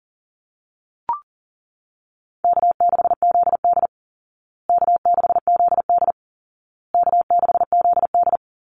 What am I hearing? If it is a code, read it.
Morse code, heard as a keyed tone, about 40 wpm, 700 Hz. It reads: K6ZD K6ZD K6ZD
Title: Easy Call Signs Repeated 3x - 40wpm